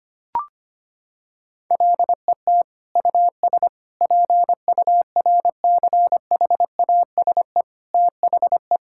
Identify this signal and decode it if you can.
Morse code, heard as a keyed tone, about 25 wpm, 700 Hz. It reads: LET US PURCHASE THE
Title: Sentences from Top 1000 Words - Review Part 6 - 25wpm